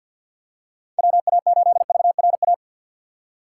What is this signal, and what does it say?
Morse code, heard as a keyed tone, about 53 wpm, 700 Hz. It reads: WA8VRA
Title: USA Call Signs - 6 Characters - 4x Character Spacing - 50wpm